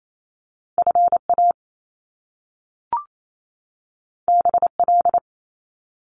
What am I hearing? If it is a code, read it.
Morse code, heard as a keyed tone, about 28 wpm, 700 Hz. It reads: F A B L
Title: Top Two Letter Combinations in Words 28wpm